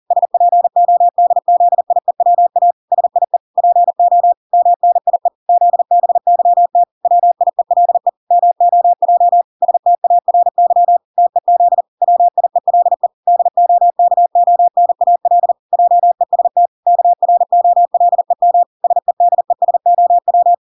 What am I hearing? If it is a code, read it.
Morse code, heard as a keyed tone, about 40 wpm, 700 Hz. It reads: SPODZIEWA SIE PO MNIE ZBYT WIELE MOJ STARY TEZ WIELE DOKONAL JEST KROLEM SEDESOW